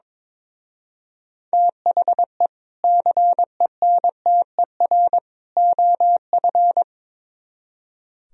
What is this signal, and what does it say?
Morse code, heard as a keyed tone, about 22 wpm, 700 Hz. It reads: THE CENTER OF
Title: Speed-Racing - Sets of 3 Words 22wpm